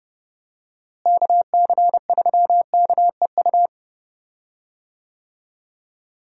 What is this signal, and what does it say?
Morse code, heard as a keyed tone, about 30 wpm, 700 Hz. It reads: KC3KEU
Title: USA Call Signs - 6 Characters - Repeated 3x - 30wpm